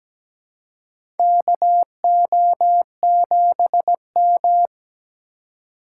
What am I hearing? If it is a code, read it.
Morse code, heard as a keyed tone, about 17 wpm, 700 Hz. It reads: KO7M
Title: USA Call Signs - 4 Characters - 2x Character Spacing - 17wpm